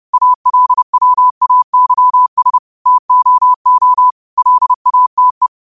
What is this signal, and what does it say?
Morse code, heard as a keyed tone, about 30 wpm, 1000 Hz. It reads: ALWAYS TOO LATE
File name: qotdfiles / 30wpm